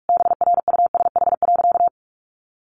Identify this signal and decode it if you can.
Morse code, heard as a keyed tone, about 44 wpm, 700 Hz. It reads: BRUSH.
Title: Sentences from Top 1100 Words 45wpm